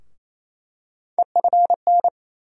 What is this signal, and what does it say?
Morse code, heard as a keyed tone, about 28 wpm, 700 Hz. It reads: EFN
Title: USA Call Sign Postfixes - 3 Letters 28wpm